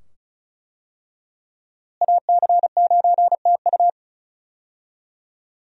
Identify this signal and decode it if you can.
Morse code, heard as a keyed tone, about 36 wpm, 700 Hz. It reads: AC9TU
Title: USA Call Signs - 5 Characters - Repeated 2x - 35wpm